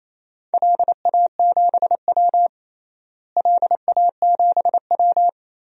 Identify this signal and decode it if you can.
Morse code, heard as a keyed tone, about 27 wpm, 700 Hz. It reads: LA7W LA7W
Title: DXCC-100 Call Signs - 4 Characters - Repeated 3x - 28wpm